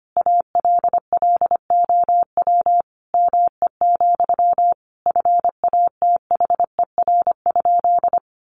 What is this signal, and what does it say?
Morse code, heard as a keyed tone, about 25 wpm, 700 Hz. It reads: ALLOW ME, FATHER?
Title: Sentences from Top 700 Words 25wpm